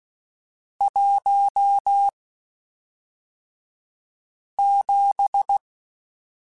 Morse code, heard as a keyed tone, about 16 wpm, spaced as for 3 wpm, 800 Hz. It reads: 17
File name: Spaced Characters / Spaced Characters-mp3s / Weather 4 21 sp @5